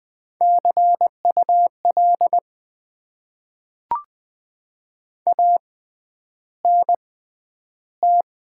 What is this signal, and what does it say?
Morse code, heard as a keyed tone, about 20 wpm, spaced as for 10 wpm, 700 Hz. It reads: CUL ANT
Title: Top 90 QSO Elements - 6x Character Spacing - 20wpm